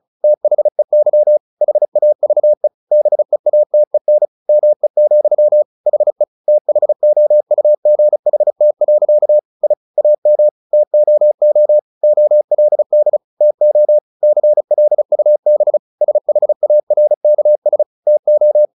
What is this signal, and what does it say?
Morse code, heard as a keyed tone, about 35 wpm, 600 Hz. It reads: THEY HAVE BEATEN ME, HE THOUGHT. I AM TOO OLD TO CLUB SHARKS TO
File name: the old man and the sea 2 35wpm